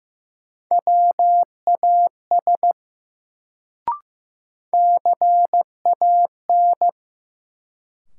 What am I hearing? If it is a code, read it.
Morse code, heard as a keyed tone, about 15 wpm, 700 Hz. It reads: WAS CAN
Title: Sentences from Top 100 Words - New Word Drill - 15wpm